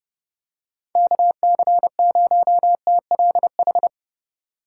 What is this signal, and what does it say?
Morse code, heard as a keyed tone, about 29 wpm, 700 Hz. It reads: KC0TLH
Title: USA Call Signs - 6 Characters - 8x Character Spacing - 30wpm